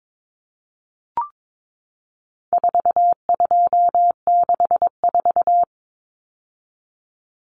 Morse code, heard as a keyed tone, about 22 wpm, 700 Hz. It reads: 4264
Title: Numbers - 4-Digit 22wpm